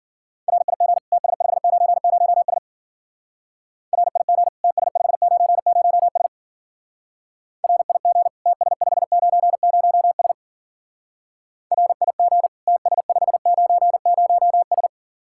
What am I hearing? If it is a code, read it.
Morse code, heard as a keyed tone, about 47 wpm, 700 Hz. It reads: RIG TS590S RIG TS590S RIG TS590S RIG TS590S
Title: Speed-Racing Common QSO Phrases - 40wpm